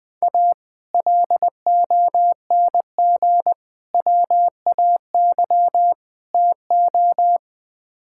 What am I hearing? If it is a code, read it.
Morse code, heard as a keyed tone, about 20 wpm, 700 Hz. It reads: A LONG WAY TO
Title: Build A Phrase - 5-Word Phrases - 20wpm